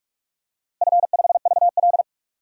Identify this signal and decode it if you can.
Morse code, heard as a keyed tone, about 46 wpm, 700 Hz. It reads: F5VL
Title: DXCC-100 Call Signs - 4 Characters - Repeated 3x - 45wpm